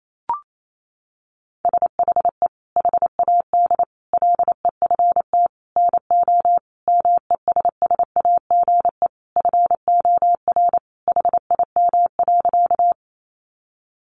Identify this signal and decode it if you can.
Morse code, heard as a keyed tone, about 28 wpm, 700 Hz. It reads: SHE HAD LEFT NO MESSAGE FOR HIM.
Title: Sentences from Top 1100 Words 28wpm